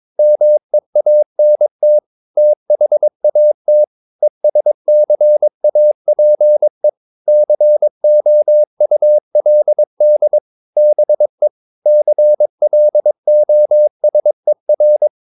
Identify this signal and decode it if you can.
Morse code, heard as a keyed tone, about 22 wpm, 600 Hz. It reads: MEANT THAT ESCAPE COULD BE CLOSER
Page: Romanen Never af Ken Follet på engelsk.